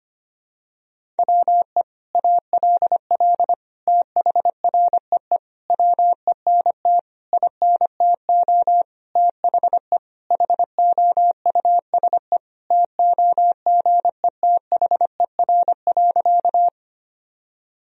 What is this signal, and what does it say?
Morse code, heard as a keyed tone, about 25 wpm, 700 Hz. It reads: WE ALL THREE WENT INTO THE HOUSE TOGETHER.
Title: Sentences from Top 1000 Words - Review Part 1 - 25wpm